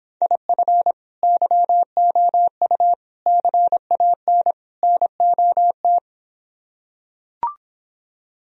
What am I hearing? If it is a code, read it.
Morse code, heard as a keyed tone, about 26 wpm, 700 Hz. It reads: IF YOU CAN NOT